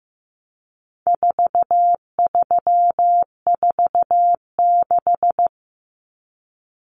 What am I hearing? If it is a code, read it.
Morse code, heard as a keyed tone, about 15 wpm, 700 Hz. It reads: 4346
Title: Numbers - 4-Digit 15wpm